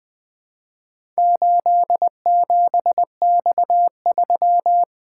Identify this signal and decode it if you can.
Morse code, heard as a keyed tone, about 20 wpm, 700 Hz. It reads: 87X3
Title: Four Letter-Number Combinations - 20wpm